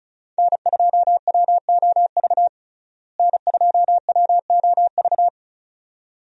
Morse code, heard as a keyed tone, about 36 wpm, 700 Hz. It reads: N2WOV N2WOV
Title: USA Call Signs - 5 Characters - Repeated 2x - 35wpm